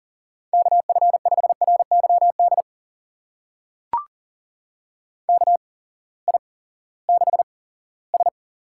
Morse code, heard as a keyed tone, about 43 wpm, spaced as for 20 wpm, 700 Hz. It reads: KF5RYD KI6S
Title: USA Call Signs - 6 Characters - 8x Character Spacing - 40wpm